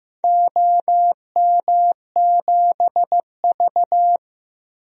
Morse code, heard as a keyed tone, about 15 wpm, 700 Hz. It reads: OM7V
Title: DXCC-100 Call Signs - 4 Characters - Repeated 2x - 15wpm